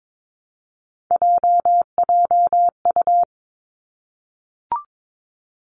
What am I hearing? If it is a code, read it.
Morse code, heard as a keyed tone, about 22 wpm, 700 Hz. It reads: JJU
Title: Three Challenging Letter-Number Combinations 22wpm